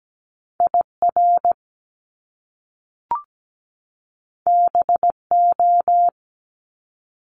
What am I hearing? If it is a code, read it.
Morse code, heard as a keyed tone, about 17 wpm, 700 Hz. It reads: IR BO
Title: Top Two Letter Combinations in Words 17wpm